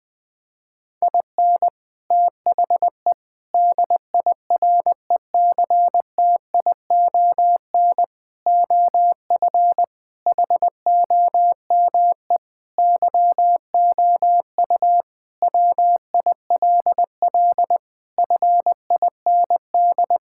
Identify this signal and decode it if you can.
Morse code, heard as a keyed tone, about 21 wpm, 700 Hz. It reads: IN THE DIRECTION OF HOME YOU WILL FIND
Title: Sentences with 4-Word Phrases - Spoken Beginning - 20wpm